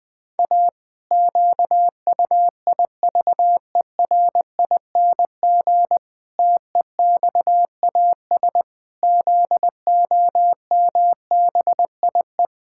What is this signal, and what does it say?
Morse code, heard as a keyed tone, about 21 wpm, 700 Hz. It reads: A QUIVERING TEXAS ZOMBIE
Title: Pangram Challenge v2 - 20wpm